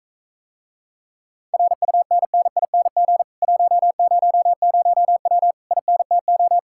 Morse code, heard as a keyed tone, about 44 wpm, 700 Hz. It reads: RUNNING 100W INTO